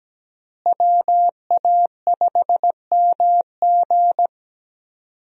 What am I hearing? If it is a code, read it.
Morse code, heard as a keyed tone, about 17 wpm, 700 Hz. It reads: WA5MG
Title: USA Call Signs - 5 Characters - Repeated 2x - 17wpm